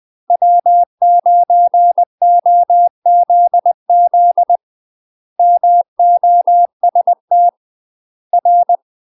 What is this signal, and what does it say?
Morse code, heard as a keyed tone, about 20 wpm, 700 Hz. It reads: W9OZZ MOST R
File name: UK MASTER QSOs TRAINING / 20WPM / UK MASTER QSOs 0006